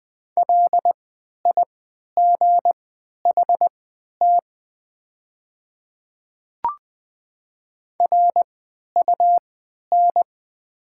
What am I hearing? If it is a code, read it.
Morse code, heard as a keyed tone, about 20 wpm, spaced as for 11 wpm, 690 Hz. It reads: LIGHT RUN
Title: 3x Character Spacing - Top 300 Words - 20wpm